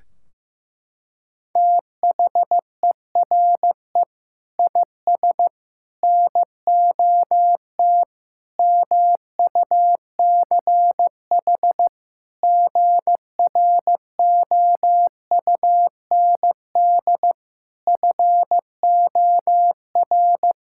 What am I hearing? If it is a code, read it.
Morse code, heard as a keyed tone, about 16 wpm, 700 Hz. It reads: THERE IS NOT MUCH GROUND FOR
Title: Sentences from Top 1600 Words 15wpm